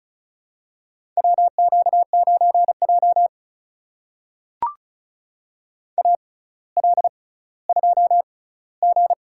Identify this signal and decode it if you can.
Morse code, heard as a keyed tone, about 36 wpm, spaced as for 18 wpm, 700 Hz. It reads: WQ9J AL2G
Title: USA Call Signs - 4 Characters - 6x Character Spacing - 35wpm